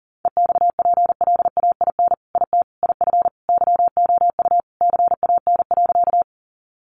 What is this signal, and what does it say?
Morse code, heard as a keyed tone, about 40 wpm, 700 Hz. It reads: EXPLAIN IT IF YOU CAN.
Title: Sentences from Top 1100 Words 40wpm